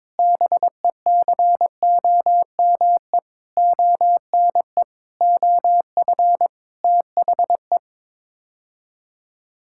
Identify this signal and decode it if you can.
Morse code, heard as a keyed tone, about 22 wpm, 700 Hz. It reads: BECOME ONE OF THE
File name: practice / 22wpm / Sets of 8 Words - 22wpm